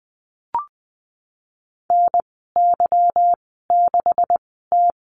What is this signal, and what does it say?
Morse code, heard as a keyed tone, about 20 wpm, spaced as for 15 wpm, 700 Hz. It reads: NY6T
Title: 186 Lesson 30g - Letter b - Call Signs 20wpm